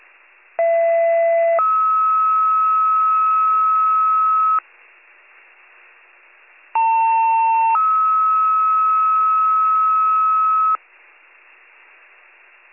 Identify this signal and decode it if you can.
Two-tone page, heard as a one-second tone, then a three-second tone, about 670 then 1290 Hz, about 900 then 1290 Hz. It is heard from 0.6 to 10.8 s.